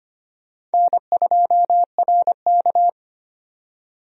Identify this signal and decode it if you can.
Morse code, heard as a keyed tone, about 25 wpm, 700 Hz. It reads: N2RK
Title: USA Call Signs - 4 Characters - 25wpm